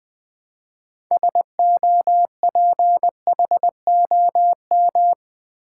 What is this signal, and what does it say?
Morse code, heard as a keyed tone, about 20 wpm, 700 Hz. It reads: SOPHOM
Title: Build A Word - 3 Letters At A Time - 9-Letter Words - 20wpm